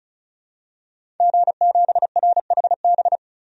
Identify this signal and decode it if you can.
Morse code, heard as a keyed tone, about 35 wpm, 700 Hz. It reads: G7RHB
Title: DXCC-100 Call Signs - 5 Characters - 35wpm